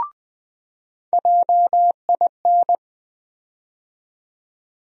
Morse code, heard as a keyed tone, about 20 wpm, 700 Hz. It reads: JIN
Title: USA Call Sign Postfixes - 3 Letters 20wpm